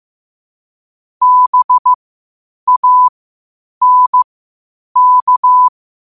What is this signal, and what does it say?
Morse code, heard as a keyed tone, about 15 wpm, spaced as for 8 wpm, 1000 Hz. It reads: BANK
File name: qotdfiles / 8wpm